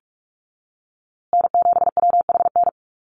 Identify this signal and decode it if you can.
Morse code, heard as a keyed tone, about 44 wpm, 700 Hz. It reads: N7WHN
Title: USA Call Signs Repeated 2x 45wpm